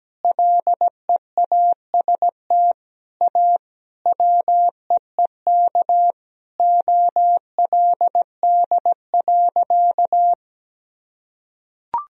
Morse code, heard as a keyed tone, about 17 wpm, 700 Hz. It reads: LEAST A WEEK OLD.
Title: Sentences from Top 1000 Words - Review Part 2 - 17wpm